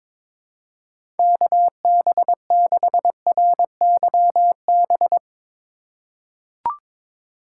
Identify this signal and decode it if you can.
Morse code, heard as a keyed tone, about 22 wpm, 700 Hz. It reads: KB6RYB
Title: USA Call Signs - 2x Character Spacing - 22wpm